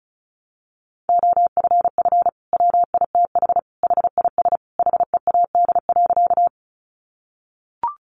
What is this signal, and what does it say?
Morse code, heard as a keyed tone, about 34 wpm, 700 Hz. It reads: OFF WITH HIS HEAD.
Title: Sentences from Top 300 Words 35wpm